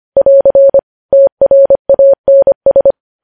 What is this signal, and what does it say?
Morse code, heard as a keyed tone, about 25 wpm, 550 Hz. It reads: <AR> TRANS
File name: podcast / News headlines-1x-25x25